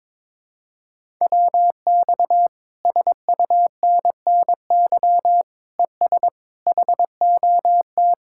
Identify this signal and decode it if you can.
Morse code, heard as a keyed tone, about 22 wpm, 700 Hz. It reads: WX SUNNY ES HOT
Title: Speed-Racing Common QSO Phrases Encore - 22wpm